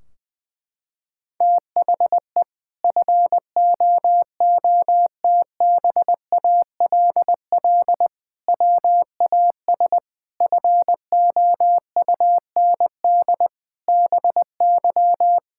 Morse code, heard as a keyed tone, about 20 wpm, 700 Hz. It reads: THE FOOTBALL WAS FOUND BY
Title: Sentences from Top 1600 Words 20wpm